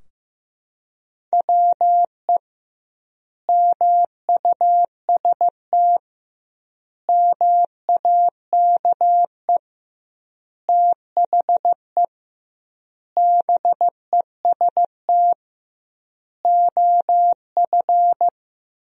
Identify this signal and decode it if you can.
Morse code, heard as a keyed tone, about 15 wpm, 700 Hz. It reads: WE MUST MAKE THE BEST OF
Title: 2x Word-Spacing - Sentences from Top 400 Words - 15wpm